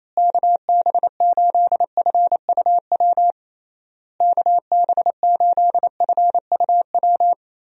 Morse code, heard as a keyed tone, about 28 wpm, 700 Hz. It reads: KB8FUW KB8FUW
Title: USA Call Signs - 6 Characters - Repeated 3x - 28wpm